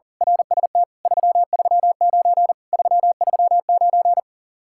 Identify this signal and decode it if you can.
Morse code, heard as a keyed tone, about 35 wpm, 700 Hz. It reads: RST 339 339
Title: Speed-Racing Common QSO Phrases - 40wpm